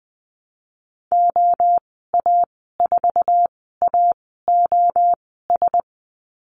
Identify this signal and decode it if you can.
Morse code, heard as a keyed tone, about 20 wpm, spaced as for 14 wpm, 700 Hz. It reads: OA4AOS
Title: 069 Lesson 12g - Letter d - Call Signs 20wpm